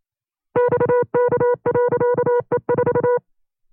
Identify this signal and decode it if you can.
Morse code, heard as a keyed tone, about 27 wpm, 490 Hz. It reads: XK.E4
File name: CW-practice/25WPM